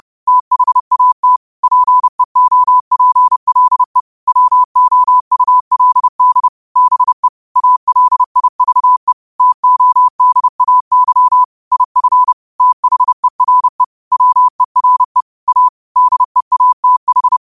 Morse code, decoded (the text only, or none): THAT PEOPLE WOULD BE ALIVE TODAY IF THERE WERE A DEATH